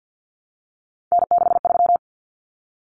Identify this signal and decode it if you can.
Morse code, heard as a keyed tone, about 50 wpm, 700 Hz. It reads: N63
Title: Common Cut Numbers - 3-Digit 50wpm